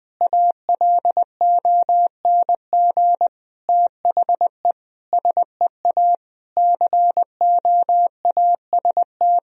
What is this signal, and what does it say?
Morse code, heard as a keyed tone, about 20 wpm, 700 Hz. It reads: ALONG THE SEA COAST
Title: Sentences from Top 1000 Words - Review Part 4 - 20wpm